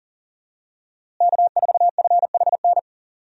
Morse code, heard as a keyed tone, about 40 wpm, 700 Hz. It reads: K4FHN
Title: USA Call Signs - 5 Characters - Repeated 2x - 40wpm